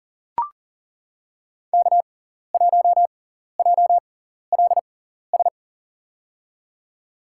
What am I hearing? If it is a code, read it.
Morse code, heard as a keyed tone, about 40 wpm, spaced as for 14 wpm, 700 Hz. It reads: K1JLS
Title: USA Call Signs - 5 Characters - 6x Character Spacing - 40wpm